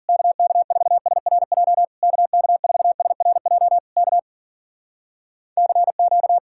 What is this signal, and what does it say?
Morse code, heard as a keyed tone, about 56 wpm, 700 Hz. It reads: KK4SRJ KK4SRJ K CQ